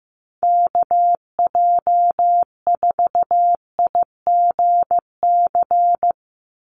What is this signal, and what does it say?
Morse code, heard as a keyed tone, about 15 wpm, 700 Hz. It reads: KJ4IGC
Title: USA Call Signs 15wpm